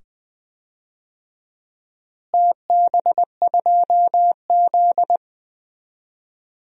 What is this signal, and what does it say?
Morse code, heard as a keyed tone, about 20 wpm, 700 Hz. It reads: TB2Z
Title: DXCC-100 Call Signs - 4 Characters - Repeated 2x - 20wpm